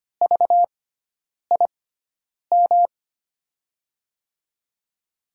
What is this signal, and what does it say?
Morse code, heard as a keyed tone, about 25 wpm, spaced as for 9 wpm, 700 Hz. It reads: VIM
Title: USA Call Signs - 6x Character Spacing - 25wpm